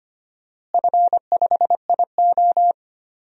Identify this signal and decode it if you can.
Morse code, heard as a keyed tone, about 25 wpm, 700 Hz. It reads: F5IO
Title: DXCC-100 Call Signs - 4 Characters - Repeated 3x - 25wpm